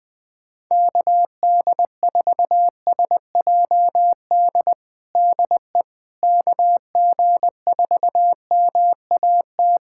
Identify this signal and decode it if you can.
Morse code, heard as a keyed tone, about 20 wpm, 700 Hz. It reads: KD4SJD DE KG4MAT
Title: USA Callsign DE Callsign 20wpm